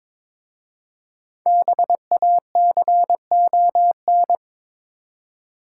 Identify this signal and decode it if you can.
Morse code, heard as a keyed tone, about 22 wpm, 700 Hz. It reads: BACON
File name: Difficult Words with Hints - 22wpm